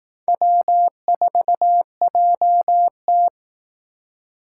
Morse code, heard as a keyed tone, about 18 wpm, 700 Hz. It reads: W4JT
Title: Speed Racing - USA 1x2 Call Signs 15wpm